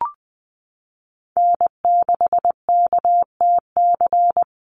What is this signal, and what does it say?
Morse code, heard as a keyed tone, about 20 wpm, 700 Hz. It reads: N6KTC
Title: USA Call Signs Repeated 3x 20wpm